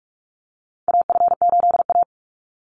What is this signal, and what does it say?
Morse code, heard as a keyed tone, about 44 wpm, 700 Hz. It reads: AF8A
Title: Call Signs - 45wpm